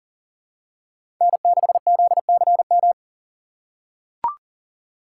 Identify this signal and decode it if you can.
Morse code, heard as a keyed tone, about 40 wpm, 700 Hz. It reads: N6ZCM
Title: USA Call Signs - 5 Characters - 4x Character Spacing - 40wpm